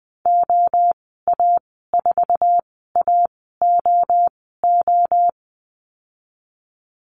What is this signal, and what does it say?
Morse code, heard as a keyed tone, about 20 wpm, spaced as for 14 wpm, 700 Hz. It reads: OA4AOO
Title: 062 Lesson 11g - Letter h - Call Signs 20wpm